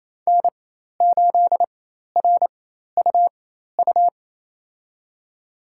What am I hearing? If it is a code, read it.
Morse code, heard as a keyed tone, about 28 wpm, spaced as for 13 wpm, 700 Hz. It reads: N8RUU
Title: USA Call Signs - 5 Characters - 4x Character Spacing - 28wpm